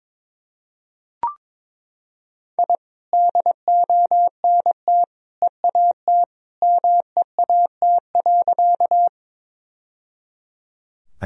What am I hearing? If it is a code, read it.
Morse code, heard as a keyed tone, about 21 wpm, 700 Hz. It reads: I DONT EAT MEAT.
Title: Sentences from Top 1000 Words - Review Part 3 - 22wpm